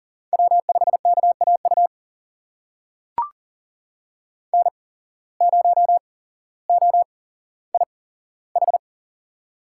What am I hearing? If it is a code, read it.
Morse code, heard as a keyed tone, about 39 wpm, spaced as for 17 wpm, 700 Hz. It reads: W5KAU N0OIH
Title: USA Call Signs - 5 Characters - 8x Character Spacing - 40wpm